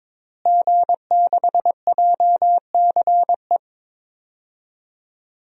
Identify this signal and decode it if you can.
Morse code, heard as a keyed tone, about 22 wpm, 700 Hz. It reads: G6JCE
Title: DXCC-100 Call Signs - 5 Characters - Repeated 2x - 22wpm